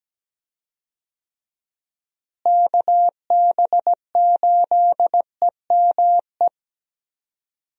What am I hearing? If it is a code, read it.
Morse code, heard as a keyed tone, about 17 wpm, 700 Hz. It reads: KB8EME
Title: USA Call Signs - 6 Characters - Repeated 3x - 17wpm